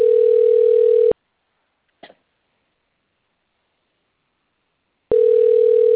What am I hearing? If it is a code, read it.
Ringback tone (North America), 440 + 480 Hz.